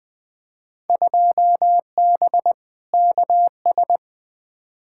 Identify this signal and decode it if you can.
Morse code, heard as a keyed tone, about 20 wpm, 700 Hz. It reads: 2B KS
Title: Field Day Exchanges Encore - 20wpm